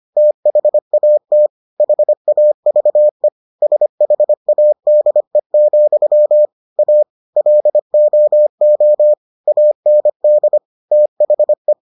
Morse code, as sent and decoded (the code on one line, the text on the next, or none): - .... .- - / .... .- ...- . / ... .... .- -.. --..-- / .- / .-.. --- --- / .- -. -.. / - .... .
THAT HAVE SHADE, A LOO AND THE